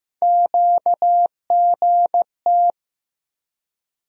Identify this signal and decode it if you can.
Morse code, heard as a keyed tone, about 15 wpm, 700 Hz. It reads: QGT
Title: USA Call Sign Postfixes - 3 Letters 15wpm